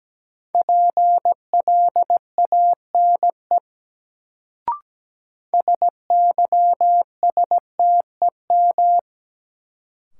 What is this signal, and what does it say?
Morse code, heard as a keyed tone, about 17 wpm, 700 Hz. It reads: PLANE SYSTEM
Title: Sentences from Top 1000 Words - Review Part 3 - New Word Drill - 17wpm